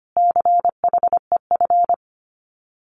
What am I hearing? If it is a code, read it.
Morse code, heard as a keyed tone, about 25 wpm, 700 Hz. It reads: CHEF
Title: Common QSO Phrases Encore 25wpm